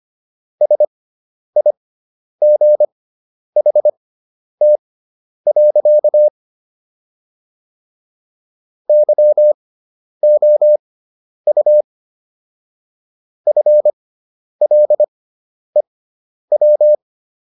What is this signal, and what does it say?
Morse code, heard as a keyed tone, about 25 wpm, spaced as for 10 wpm, 600 Hz. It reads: SIGHT. YOU FLEW